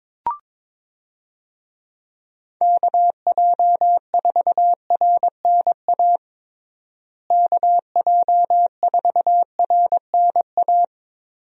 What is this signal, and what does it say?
Morse code, heard as a keyed tone, about 22 wpm, 700 Hz. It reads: KJ4RNA KJ4RNA
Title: USA Call Signs - 6 Characters - Repeated 2x - 22wpm